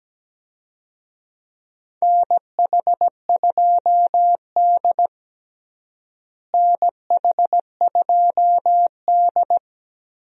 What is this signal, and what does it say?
Morse code, heard as a keyed tone, about 17 wpm, 700 Hz. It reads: NH2D NH2D
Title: USA Call Signs - 4 Characters - Repeated 3x - 17wpm